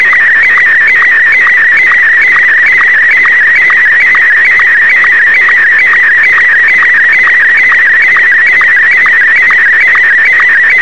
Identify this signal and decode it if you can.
SSTV (slow-scan television), Martin 1.